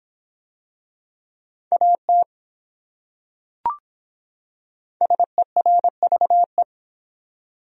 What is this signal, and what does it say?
Morse code, heard as a keyed tone, about 26 wpm, 700 Hz. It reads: AT SERVE